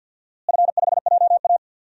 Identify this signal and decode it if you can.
Morse code, heard as a keyed tone, about 52 wpm, 700 Hz. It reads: F5JA
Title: DXCC-100 Call Signs - 4 Characters - Repeated 3x - 50wpm